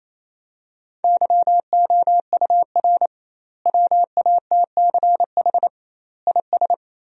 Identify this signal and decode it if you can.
Morse code, heard as a keyed tone, about 28 wpm, 700 Hz. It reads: YOUR WATCH IS
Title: 2x Word-Spacing - Sentences from Top 400 Words - 28wpm